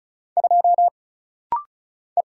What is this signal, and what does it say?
Morse code, heard as a keyed tone, about 35 wpm, 700 Hz. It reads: 2 E
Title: Single Letter-Number - Rapid-Fire 35wpm